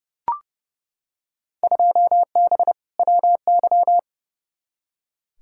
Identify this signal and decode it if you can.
Morse code, heard as a keyed tone, about 32 wpm, 700 Hz. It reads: 2B WY
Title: Field Day Exchanges Encore - 30wpm